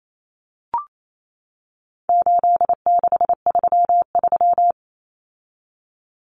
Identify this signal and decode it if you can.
Morse code, heard as a keyed tone, about 28 wpm, 700 Hz. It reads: 8633
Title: Numbers - 4-Digit 28wpm